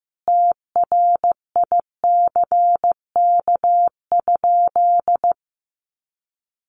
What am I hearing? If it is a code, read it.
Morse code, heard as a keyed tone, about 15 wpm, 700 Hz. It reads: TRICK?
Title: Sentences from Top 1500 Words 15wpm